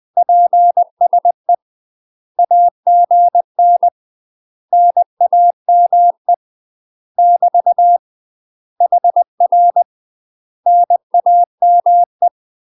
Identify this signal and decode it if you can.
Morse code, heard as a keyed tone, about 20 wpm, 700 Hz. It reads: PSE AGN NAME <BT> HR NAME
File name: UK MASTER QSOs TRAINING / 20WPM / UK MASTER QSOs 0011